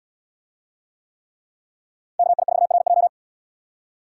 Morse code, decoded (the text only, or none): KI4RJ